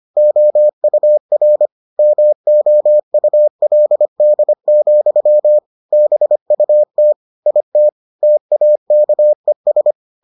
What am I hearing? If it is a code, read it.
Morse code, heard as a keyed tone, about 25 wpm, 600 Hz. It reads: OUR MOULD, BUT IT TAKES